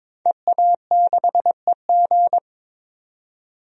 Morse code, heard as a keyed tone, about 22 wpm, 700 Hz. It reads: EA6EG
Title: DXCC-100 Call Signs - 5 Characters - 22wpm